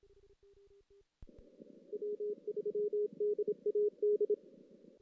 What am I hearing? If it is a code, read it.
Morse code, heard as a keyed tone, about 28 wpm, 410 Hz. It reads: W3DAD
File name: 2016 JAN CW NAQP / 2016 Jan naqp k5ot - 0-5